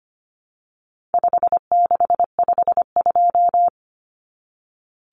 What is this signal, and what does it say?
Morse code, heard as a keyed tone, about 25 wpm, 700 Hz. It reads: 5652
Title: Common Cut Numbers - 4-Digit 25wpm